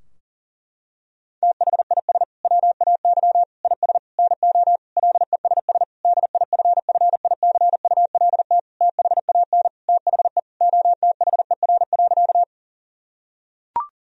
Morse code, heard as a keyed tone, about 40 wpm, 700 Hz. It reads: THIS WAY IS NO LESS DIFFICULT THAN THE OTHER.
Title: Sentences from Top 1000 Words - Review Part 2 - 40wpm